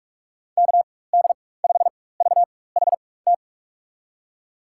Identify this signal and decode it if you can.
Morse code, heard as a keyed tone, about 47 wpm, spaced as for 21 wpm, 700 Hz. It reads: KD5VHT
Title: USA Call Signs - 6 Characters - 4x Character Spacing - 45wpm